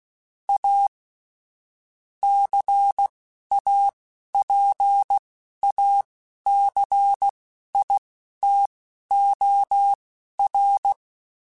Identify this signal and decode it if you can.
Morse code, heard as a keyed tone, about 16 wpm, spaced as for 11 wpm, 800 Hz. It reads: A CAPACITOR
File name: Technical Subjects / Technical Subjects-mp3s / XC @10